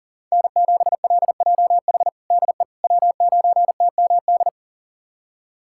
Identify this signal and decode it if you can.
Morse code, heard as a keyed tone, about 40 wpm, 700 Hz. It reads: N7LJH DE W9TMD
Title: USA Callsign DE Callsign - 5 Characters - 40wpm